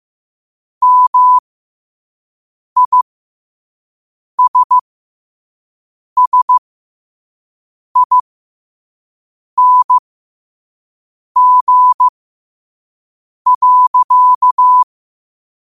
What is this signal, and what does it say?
Morse code, heard as a keyed tone, about 15 wpm, spaced as for 5 wpm, 1000 Hz. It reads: MISSING.
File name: qotdfiles / 5wpm